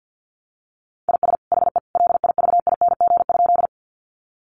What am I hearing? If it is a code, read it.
Morse code, heard as a keyed tone, about 50 wpm, 700 Hz. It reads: IS HE LIVING?
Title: Sentences from Top 1100 Words 50wpm